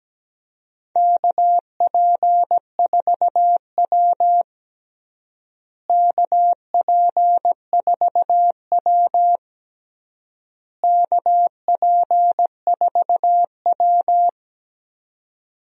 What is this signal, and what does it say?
Morse code, heard as a keyed tone, about 17 wpm, 700 Hz. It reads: KP4W KP4W KP4W
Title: DXCC-100 Call Signs - 4 Characters - Repeated 3x - 17wpm